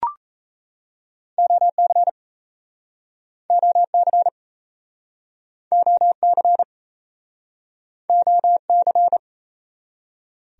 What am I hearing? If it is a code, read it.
Morse code, heard as a keyed tone, about 38 wpm, 700 Hz. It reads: OC OC OC OC